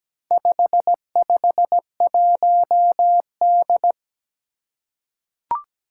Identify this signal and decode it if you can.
Morse code, heard as a keyed tone, about 17 wpm, 700 Hz. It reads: 551D
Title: Four Challenging Letter-Number Combinations 17wpm